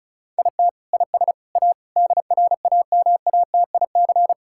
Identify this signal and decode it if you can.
Morse code, heard as a keyed tone, about 36 wpm, 700 Hz. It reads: IT IS A DRAMATIC
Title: Sentences from Top 1600 Words 35wpm